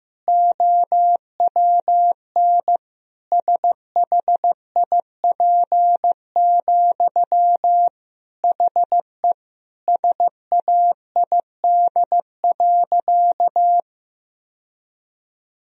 Morse code, heard as a keyed tone, about 16 wpm, 700 Hz. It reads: OWN SHIP, HE SAID.
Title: Sentences from Top 1000 Words - Review Part 2 - 15wpm